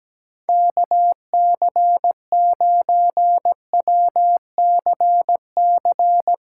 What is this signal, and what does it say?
Morse code, heard as a keyed tone, about 20 wpm, 700 Hz. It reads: KC9WCC
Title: Speed Racing - USA 2x3 Call Signs 17wpm